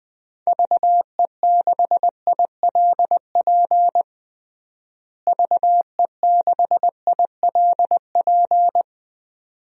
Morse code, heard as a keyed tone, about 21 wpm, 700 Hz. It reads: VE6ILP VE6ILP
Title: DXCC-100 Call Signs - 6 Characters - Repeated 2x - 20wpm